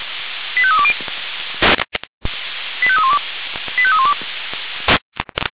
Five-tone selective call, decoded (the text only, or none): 85320 85312 85312